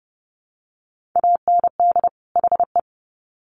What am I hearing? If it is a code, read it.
Morse code, heard as a keyed tone, about 30 wpm, 700 Hz. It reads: AND HE
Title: Sets of 2 Words 30wpm